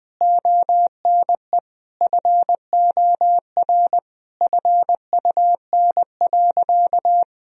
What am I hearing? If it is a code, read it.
Morse code, heard as a keyed tone, about 19 wpm, 700 Hz. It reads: ONE FOR FUN.
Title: Sentences from Top 1000 Words - Review Part 3 - 20wpm